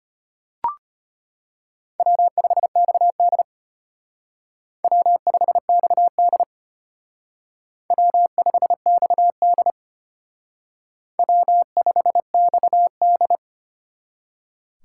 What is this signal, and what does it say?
Morse code, heard as a keyed tone, about 32 wpm, 700 Hz. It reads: W5XD W5XD W5XD W5XD